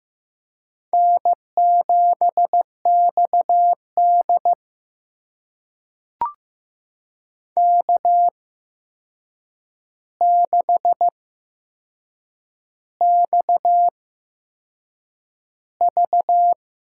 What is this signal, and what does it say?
Morse code, heard as a keyed tone, about 15 wpm, spaced as for 6 wpm, 700 Hz. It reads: N7XD K6XV
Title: USA Call Signs - 4 Characters - 8x Character Spacing - 15wpm